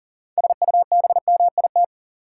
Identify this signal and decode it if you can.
Morse code, heard as a keyed tone, about 40 wpm, 700 Hz. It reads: SUBMIT
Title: Six Letter Words - 40wpm